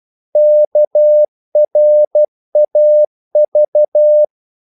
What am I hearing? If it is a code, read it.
Morse code, heard as a keyed tone, about 12 wpm, 600 Hz. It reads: KRAV
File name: Korte ord DK 12wpm